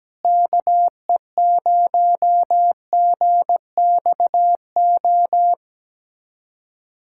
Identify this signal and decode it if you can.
Morse code, heard as a keyed tone, about 17 wpm, 700 Hz. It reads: KE0GXO
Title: USA Callsign DE Callsign - 6 Characters - 17wpm